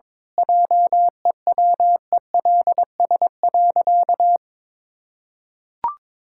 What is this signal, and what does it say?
Morse code, heard as a keyed tone, about 22 wpm, 700 Hz. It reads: JEWELS.
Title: Pangram Challenge v2 - 22wpm